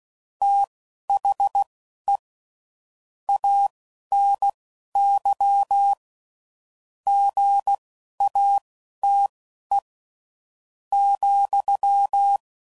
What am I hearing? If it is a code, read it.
Morse code, heard as a keyed tone, about 16 wpm, spaced as for 11 wpm, 800 Hz. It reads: THE ANY GATE ,
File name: CW Code Groups @10WPM / Technical Subjects / OR Gate